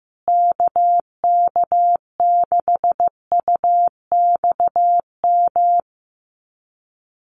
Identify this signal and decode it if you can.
Morse code, heard as a keyed tone, about 15 wpm, 700 Hz. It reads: KK6UXM
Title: USA Call Signs Repeated 2x 15wpm